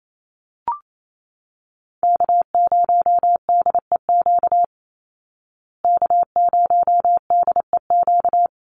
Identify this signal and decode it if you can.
Morse code, heard as a keyed tone, about 28 wpm, 700 Hz. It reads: K0DEQ K0DEQ
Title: Call Signs Repeated 2x - 28wpm